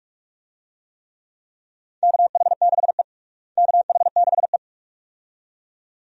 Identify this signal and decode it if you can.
Morse code, heard as a keyed tone, about 46 wpm, 700 Hz. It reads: KH6E KH6E
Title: DXCC-100 Call Signs - 4 Characters - Repeated 2x - 45wpm